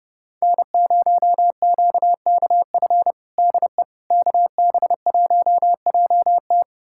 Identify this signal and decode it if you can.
Morse code, heard as a keyed tone, about 30 wpm, 700 Hz. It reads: N0QKF DE KB1JT
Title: USA Callsign DE Callsign - 5 Characters - 30wpm